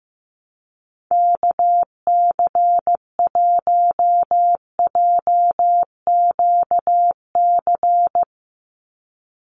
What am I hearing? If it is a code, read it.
Morse code, heard as a keyed tone, about 15 wpm, 700 Hz. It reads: KC1JQC
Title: USA Call Signs 15wpm